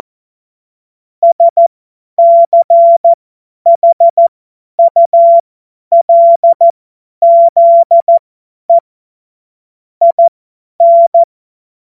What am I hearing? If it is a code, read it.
Morse code, heard as a keyed tone, about 14 wpm, spaced as for 10 wpm, 680 Hz. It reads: SCHULZE IN
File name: Morsekurs / Witze / Witz 15. November 2022 50 F70